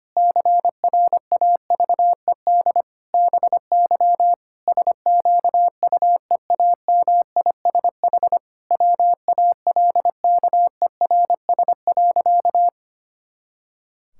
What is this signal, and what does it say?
Morse code, heard as a keyed tone, about 25 wpm, 700 Hz. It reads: CRAVED BY SQUEAMISH WALKERS.
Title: Pangram Challenge v2 - 25wpm